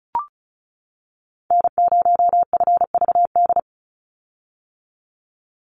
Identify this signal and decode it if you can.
Morse code, heard as a keyed tone, about 34 wpm, 700 Hz. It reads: N0FVD
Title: USA Call Signs 35wpm